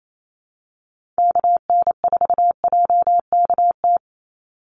Morse code, heard as a keyed tone, about 28 wpm, 700 Hz. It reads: KN4JKT
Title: USA Call Signs 28wpm